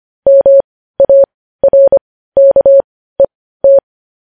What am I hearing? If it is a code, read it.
Morse code, heard as a keyed tone, about 25 wpm, spaced as for 15 wpm, 550 Hz. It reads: MARKET
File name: podcast / News headlines-1x-25x15